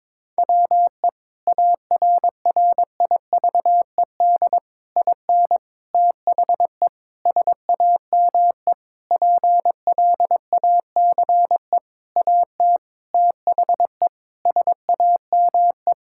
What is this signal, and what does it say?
Morse code, heard as a keyed tone, about 22 wpm, 700 Hz. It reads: WE ARRIVED IN THE SAME PLACE AT THE SAME
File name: practice / 22wpm / Sentences with 8-Word Phrases - 22wpm